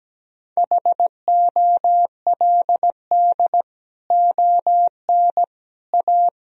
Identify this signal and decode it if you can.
Morse code, heard as a keyed tone, about 17 wpm, 700 Hz. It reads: HOLD ON A
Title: Sentences from Top 1000 Words - Review Part 2 - 17wpm